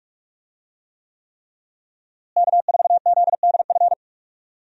Morse code, heard as a keyed tone, about 45 wpm, 700 Hz. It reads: K4ZDF
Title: USA Call Signs - 5 Characters - Repeated 3x - 45wpm